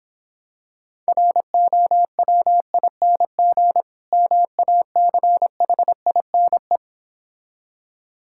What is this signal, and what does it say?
Morse code, heard as a keyed tone, about 26 wpm, 700 Hz. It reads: ROWING MACHINE